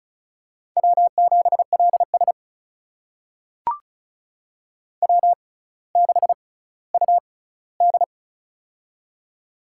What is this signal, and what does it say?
Morse code, heard as a keyed tone, about 36 wpm, spaced as for 18 wpm, 700 Hz. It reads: W7LS W6UD
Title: USA Call Signs - 4 Characters - 6x Character Spacing - 35wpm